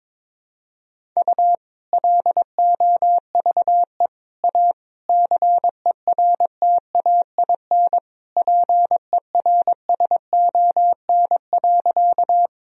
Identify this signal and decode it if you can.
Morse code, heard as a keyed tone, about 22 wpm, 700 Hz. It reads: U LOVE A CERTAIN PERSON.
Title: Sentences from Top 500 Words and Abbreviations 22wpm